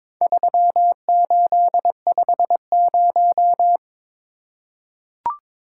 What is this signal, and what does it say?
Morse code, heard as a keyed tone, about 22 wpm, 700 Hz. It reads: 3850
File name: practice / 22wpm / Four Letter-Number Combinations - 22wpm